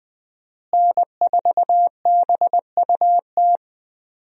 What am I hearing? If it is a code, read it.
Morse code, heard as a keyed tone, about 20 wpm, 700 Hz. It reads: N4BUT
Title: USA Call Signs - 5 Characters - Repeated 3x - 20wpm